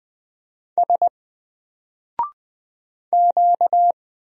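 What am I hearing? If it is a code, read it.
Morse code, heard as a keyed tone, about 20 wpm, 700 Hz. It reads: S Q
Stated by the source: Single letter and number Morse drill at 20 wpm, from N4LCD.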